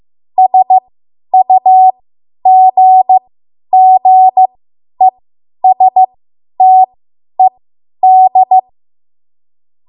Morse code, heard as a keyed tone, about 15 wpm, spaced as for 10 wpm, 750 Hz. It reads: SUGGESTED
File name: Morse / 10WPM